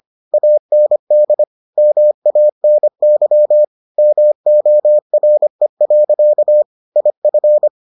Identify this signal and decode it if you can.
Morse code, heard as a keyed tone, about 25 wpm, 600 Hz. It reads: AND MANY MORE. IF